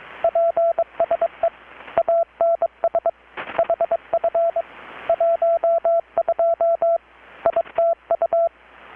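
Morse code, heard as a keyed tone, about 23 wpm, 670 Hz, with radio noise. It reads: PSE ANS HF 12 ITU
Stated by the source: Morse from KPH's Night of Nights XX maritime radiotelegraphy event, received in Chiapas, Mexico about 3,800 km away on a Kenwood R-600 with a grounded 10 m long-wire antenna.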